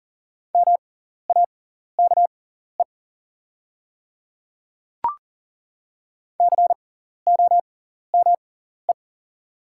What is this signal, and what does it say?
Morse code, heard as a keyed tone, about 38 wpm, spaced as for 14 wpm, 690 Hz. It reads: MAKE COME
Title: 6x Character Spacing - Top 100 Words - 40wpm